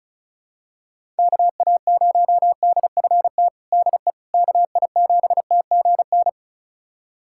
Morse code, heard as a keyed tone, about 34 wpm, 700 Hz. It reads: KA0DFT DE KI7TGN
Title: USA Callsign DE Callsign - 6 Characters - 35wpm